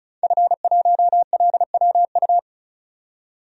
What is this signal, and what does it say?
Morse code, heard as a keyed tone, about 36 wpm, 700 Hz. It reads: F1LWU
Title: DXCC-100 Call Signs - 5 Characters - 35wpm